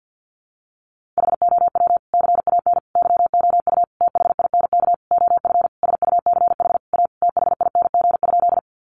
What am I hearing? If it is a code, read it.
Morse code, heard as a keyed tone, about 48 wpm, 700 Hz. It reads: HOW CAN YOU THINK OF SUCH A THING?
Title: Sentences from Top 200 Words 50wpm